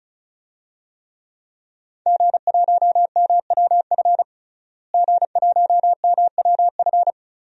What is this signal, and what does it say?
Morse code, heard as a keyed tone, about 36 wpm, 700 Hz. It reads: G1MWF G1MWF
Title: DXCC-100 Call Signs - 5 Characters - Repeated 3x - 35wpm